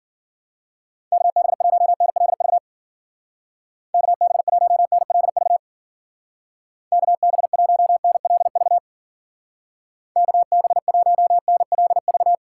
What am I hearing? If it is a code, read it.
Morse code, heard as a keyed tone, about 50 wpm, 700 Hz. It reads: KB1NLV KB1NLV KB1NLV KB1NLV
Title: Speed Racing - USA Call Signs 40wpm